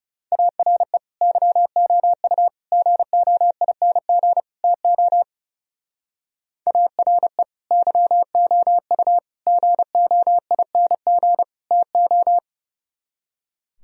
Morse code, heard as a keyed tone, about 36 wpm, 700 Hz. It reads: ARE YOU GOING TO ARE YOU GOING TO
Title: Speed-Racing - Sets of 4 Words 30wpm